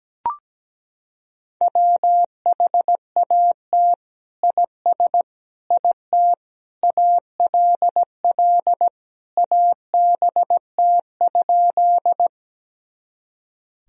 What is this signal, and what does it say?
Morse code, heard as a keyed tone, about 17 wpm, 700 Hz. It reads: WHAT IS IT ALL ABT?
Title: Sentences from Top 100 Words and Abbreviations 17wpm